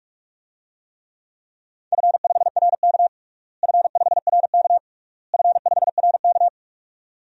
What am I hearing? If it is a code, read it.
Morse code, heard as a keyed tone, about 43 wpm, 700 Hz. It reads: F5RK F5RK F5RK
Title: DXCC-100 Call Signs - 4 Characters - Repeated 3x - 45wpm